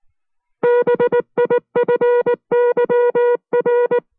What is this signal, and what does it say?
Morse code, heard as a keyed tone, about 19 wpm, 480 Hz. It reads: BIFYR